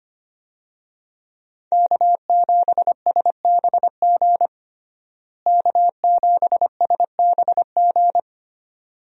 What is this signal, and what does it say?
Morse code, heard as a keyed tone, about 25 wpm, 700 Hz. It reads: K7SBG K7SBG
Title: USA Call Signs - 5 Characters - Repeated 2x - 25wpm